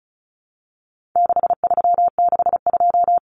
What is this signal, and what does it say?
Morse code, heard as a keyed tone, about 34 wpm, 700 Hz. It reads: 6362
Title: Numbers - 4-Digit 35wpm